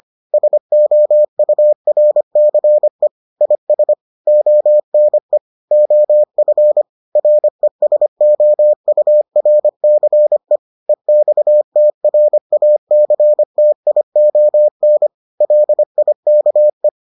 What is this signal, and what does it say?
Morse code, heard as a keyed tone, about 25 wpm, 600 Hz. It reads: SOURCE IS ONE OF RESOURCE EXTRACTION LIKE